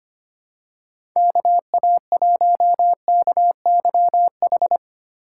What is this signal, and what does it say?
Morse code, heard as a keyed tone, about 25 wpm, 700 Hz. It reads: KA1KYH
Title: USA Call Signs - 6 Characters - 4x Character Spacing - 25wpm